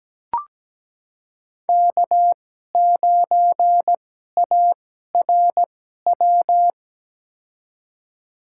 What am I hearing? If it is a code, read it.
Morse code, heard as a keyed tone, about 17 wpm, spaced as for 12 wpm, 700 Hz. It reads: K9ARW
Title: USA Call Signs - 5 Characters - 2x Character Spacing - 17wpm